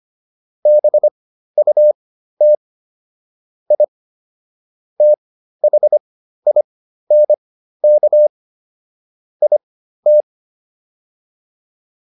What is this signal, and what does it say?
Morse code, heard as a keyed tone, about 25 wpm, spaced as for 13 wpm, 600 Hz. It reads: BUT I THINK IT